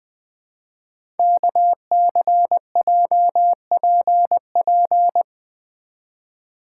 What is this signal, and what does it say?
Morse code, heard as a keyed tone, about 20 wpm, 700 Hz. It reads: KCJPP
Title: Code Groups - 5 Letters - 20wpm